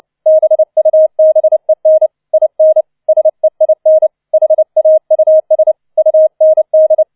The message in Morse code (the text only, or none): BUBEN IN SEIN HAUS UND